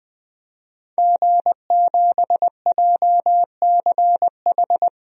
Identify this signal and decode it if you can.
Morse code, heard as a keyed tone, about 21 wpm, 700 Hz. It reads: G7JCH
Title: DXCC-100 Call Signs - 5 Characters - 20wpm